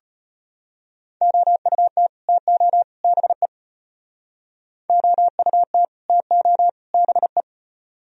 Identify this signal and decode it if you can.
Morse code, heard as a keyed tone, about 30 wpm, 700 Hz. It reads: OUT TO BE OUT TO BE
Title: Speed-Racing - Sets of 3 Words 25wpm